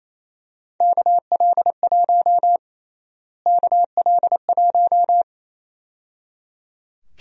Morse code, heard as a keyed tone, about 28 wpm, 700 Hz. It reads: KL1 KL1
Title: USA Call Sign Prefixes - Repeated 2x - 28wpm